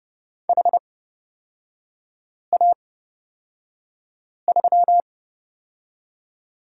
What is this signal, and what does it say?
Morse code, heard as a keyed tone, about 30 wpm, spaced as for 5 wpm, 700 Hz. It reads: HA3
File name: Lesson 38e - Letter q - Three Characters 30wpm